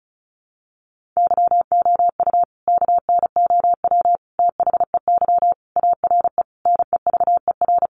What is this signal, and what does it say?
Morse code, heard as a keyed tone, about 34 wpm, 700 Hz. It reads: YOU KNOW THEY ARE NEVER
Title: Sentences from Top 1400 Words 35wpm